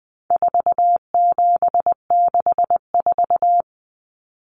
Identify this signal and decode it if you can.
Morse code, heard as a keyed tone, about 20 wpm, 700 Hz. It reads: 4764
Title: Numbers - 4-Digit 20wpm